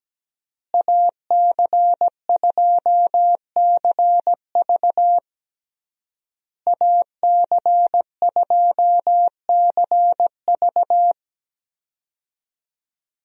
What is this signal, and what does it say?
Morse code, heard as a keyed tone, about 17 wpm, 700 Hz. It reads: AC2CV AC2CV
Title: USA Call Signs - 5 Characters - Repeated 3x - 17wpm